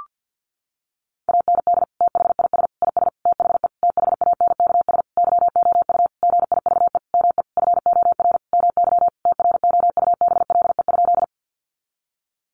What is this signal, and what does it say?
Morse code, heard as a keyed tone, about 50 wpm, 700 Hz. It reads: AND THIS IS THE THANKS YOU GIVE ME FOR MY TROUBLE?
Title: Sentences from Top 1200 Words 50wpm